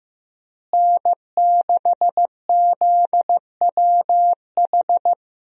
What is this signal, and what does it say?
Morse code, heard as a keyed tone, about 15 wpm, 700 Hz. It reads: N6ZWH
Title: USA Call Signs - 5 Characters - Repeated 2x - 15wpm